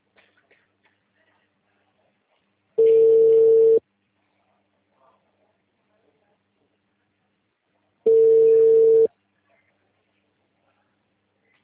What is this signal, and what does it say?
Ringback tone (China, 450 Hz).
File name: out- -2020- - - .14216